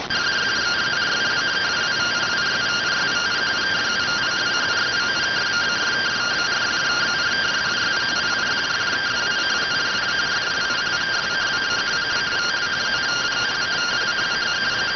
Two-tone teleprinter signal, about 170 Hz shift.